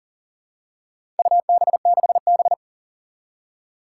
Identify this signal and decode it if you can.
Morse code, heard as a keyed tone, about 40 wpm, 700 Hz. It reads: UB6B
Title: Four Challenging Letter-Number Combinations 40wpm